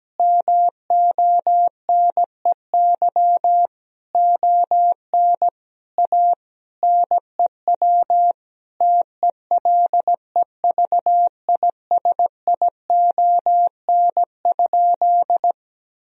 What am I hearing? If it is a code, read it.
Morse code, heard as a keyed tone, about 17 wpm, 700 Hz. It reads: MONEY ON A NEW TELEVISION?
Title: Sentences from Top 1000 Words - Review Part 4 - 17wpm